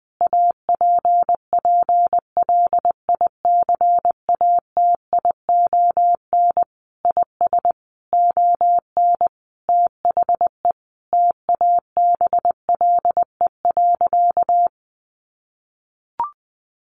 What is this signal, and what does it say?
Morse code, heard as a keyed tone, about 20 wpm, 700 Hz. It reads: APPLICATION IS ON THE TABLE.
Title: Sentences from Top 1200 Words 20wpm